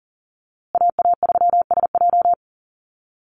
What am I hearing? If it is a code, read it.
Morse code, heard as a keyed tone, about 40 wpm, 700 Hz. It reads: AA3SJ
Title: USA Call Signs Repeated 3x 40wpm